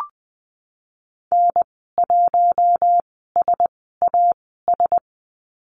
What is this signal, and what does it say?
Morse code, heard as a keyed tone, about 20 wpm, spaced as for 15 wpm, 700 Hz. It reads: N1SAS
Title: 042 Lesson 8g - Number 1 - Call Signs 20wpm